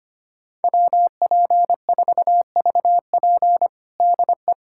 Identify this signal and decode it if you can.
Morse code, heard as a keyed tone, about 25 wpm, 700 Hz. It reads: WP4VP DE
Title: USA Callsign DE Callsign - 5 Characters - 25wpm